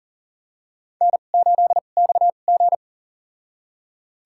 Morse code, heard as a keyed tone, about 40 wpm, spaced as for 29 wpm, 700 Hz. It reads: N8XG
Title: USA Call Signs - 4 Characters - 2x Character Spacing - 40wpm